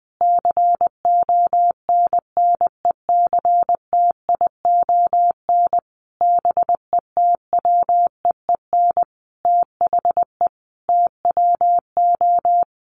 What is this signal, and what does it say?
Morse code, heard as a keyed tone, about 19 wpm, 700 Hz. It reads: CONNECTION BETWEEN THE TWO
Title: Sentences from Top 1300 Words 20wpm